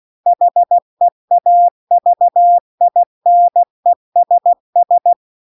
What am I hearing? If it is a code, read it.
Morse code, heard as a keyed tone, about 16 wpm, 700 Hz. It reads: HEAVINESS
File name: CW-The-Big-Twoheaded-River-2-16WPM